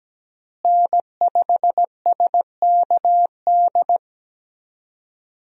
Morse code, heard as a keyed tone, about 17 wpm, 700 Hz. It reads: N5SKD
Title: USA Call Signs - 5 Characters - Repeated 2x - 17wpm